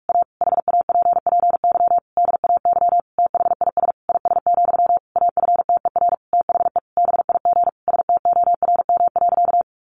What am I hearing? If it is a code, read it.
Morse code, heard as a keyed tone, about 44 wpm, 700 Hz. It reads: A HAPPY DAY THIS IS, AFTER THE BIG STORM.
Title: Sentences from Top 1500 Words 45wpm